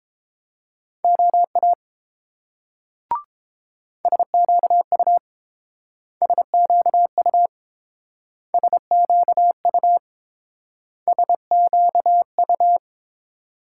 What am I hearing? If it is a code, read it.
Morse code, heard as a keyed tone, about 30 wpm, 700 Hz. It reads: OA SQU SQU SQU SQU